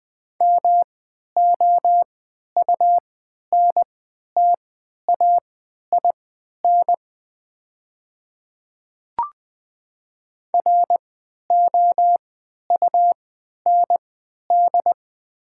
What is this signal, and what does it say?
Morse code, heard as a keyed tone, about 20 wpm, spaced as for 11 wpm, 700 Hz. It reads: MOUNTAIN ROUND
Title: 3x Character Spacing - Top 300 Words - 20wpm